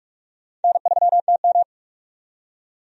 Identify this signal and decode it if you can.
Morse code, heard as a keyed tone, about 46 wpm, 700 Hz. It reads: N3TM
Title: USA Call Signs - 4 Characters - 45wpm